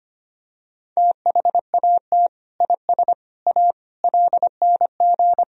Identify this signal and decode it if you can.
Morse code, heard as a keyed tone, about 25 wpm, 700 Hz. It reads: THAT IS A LNG
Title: Sentences from Top 100 Words and Abbreviations 25wpm